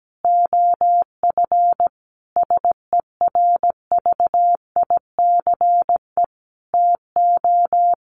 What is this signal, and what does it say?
Morse code, heard as a keyed tone, about 17 wpm, 700 Hz. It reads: OF SERVICE TO
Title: Sentences from Top 1200 Words 17wpm